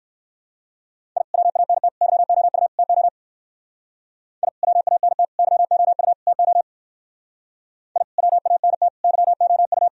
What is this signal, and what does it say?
Morse code, heard as a keyed tone, about 52 wpm, 700 Hz. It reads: I WANT YOU TO I WANT YOU TO I WANT YOU
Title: Speed-Racing - Sets of 4 Words 45wpm